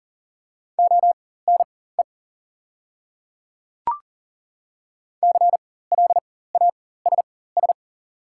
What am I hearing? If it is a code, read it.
Morse code, heard as a keyed tone, about 40 wpm, spaced as for 19 wpm, 700 Hz. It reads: ONE CLASS
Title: 4x Character Spacing - Top 500 Words - 40wpm